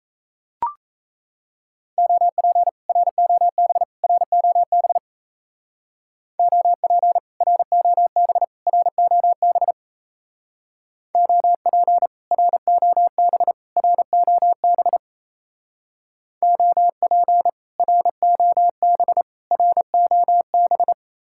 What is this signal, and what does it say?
Morse code, heard as a keyed tone, about 34 wpm, 700 Hz. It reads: OP ROB ROB OP ROB ROB OP ROB ROB OP ROB ROB
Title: Speed-Racing Common QSO Phrases - 28wpm